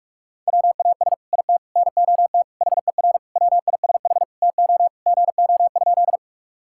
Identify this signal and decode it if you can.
Morse code, heard as a keyed tone, about 46 wpm, 700 Hz. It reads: WAS IT NOT HER WISH TO GO?
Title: Sentences from Top 1600 Words 45wpm